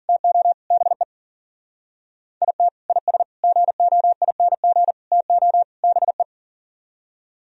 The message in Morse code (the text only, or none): TO BE IT IS GOING TO BE